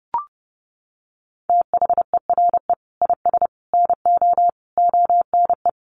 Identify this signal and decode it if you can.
Morse code, heard as a keyed tone, about 30 wpm, 700 Hz. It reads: THERE IS NO ONE
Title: Sentences from Top 200 Words 30wpm